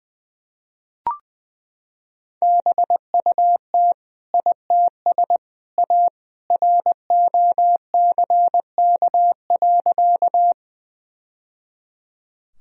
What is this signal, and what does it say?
Morse code, heard as a keyed tone, about 20 wpm, 700 Hz. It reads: BUT ITS A ROCK.
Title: Sentences from Top 1000 Words - Review Part 2 - 20wpm